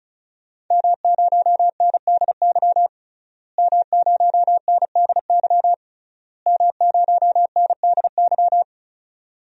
Morse code, heard as a keyed tone, about 36 wpm, 700 Hz. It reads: M0NDY M0NDY M0NDY
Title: DXCC-100 Call Signs - 5 Characters - Repeated 3x - 35wpm